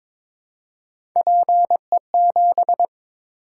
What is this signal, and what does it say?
Morse code, heard as a keyed tone, about 22 wpm, 700 Hz. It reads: PE7
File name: practice / 22wpm / Three Letter-Number Combinations - 22wpm